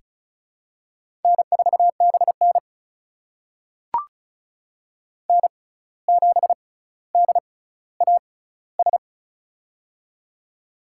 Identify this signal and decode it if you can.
Morse code, heard as a keyed tone, about 36 wpm, spaced as for 17 wpm, 700 Hz. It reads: N4BN N7DAS
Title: USA Call Signs - 6x Character Spacing - 35wpm